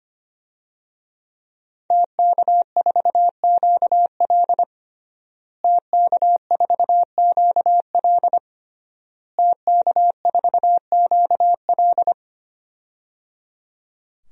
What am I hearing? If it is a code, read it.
Morse code, heard as a keyed tone, about 25 wpm, 700 Hz. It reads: TK4QL TK4QL TK4QL
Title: DXCC-100 Call Signs - 5 Characters - Repeated 3x - 25wpm